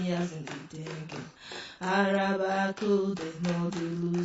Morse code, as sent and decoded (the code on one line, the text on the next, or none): none